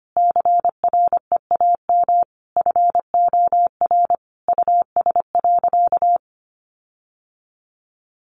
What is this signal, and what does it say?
Morse code, heard as a keyed tone, about 25 wpm, 700 Hz. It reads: CREAM FOR US.
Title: Sentences from Top 1500 Words 25wpm